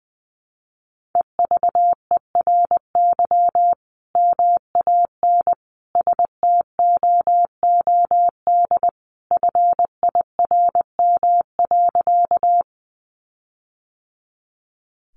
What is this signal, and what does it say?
Morse code, heard as a keyed tone, about 20 wpm, 700 Hz. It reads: EVERY MAN STOOD FIRM.
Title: Sentences from Top 1400 Words 20wpm